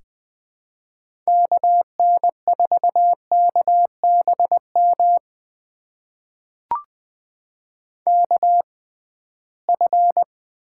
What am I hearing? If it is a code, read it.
Morse code, heard as a keyed tone, about 21 wpm, spaced as for 15 wpm, 700 Hz. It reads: KN4KBM KF
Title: USA Call Signs - 6 Characters - 6x Character Spacing - 20wpm